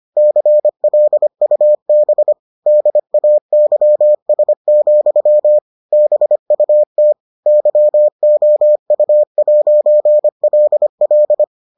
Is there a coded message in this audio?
Yes: Morse code from 0.2 to 11.4 s, heard as a keyed tone, about 25 wpm, 600 Hz. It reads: CLUB DAYS, BUT YOU'LL